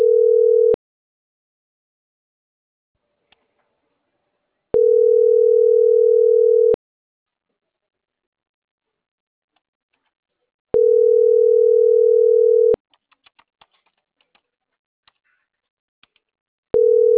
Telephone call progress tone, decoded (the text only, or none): ringback tone